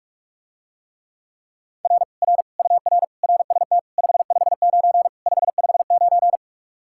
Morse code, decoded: R R UR RST 559 559

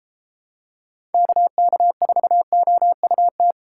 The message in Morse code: KK4OUT

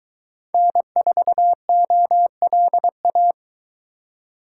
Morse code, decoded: N4OLA

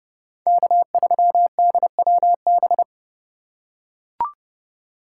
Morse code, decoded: K3DWB

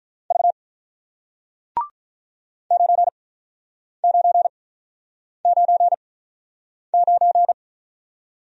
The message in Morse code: V 9 9 9 9

